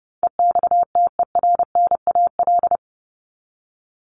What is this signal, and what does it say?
Morse code, heard as a keyed tone, about 30 wpm, 700 Hz. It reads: EXTERNAL
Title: Top 1751-2000 Words 30wpm